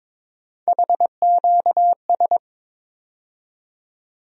Morse code, heard as a keyed tone, about 22 wpm, 700 Hz. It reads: HQS
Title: USA Call Sign Postfixes - 3 Letters 22wpm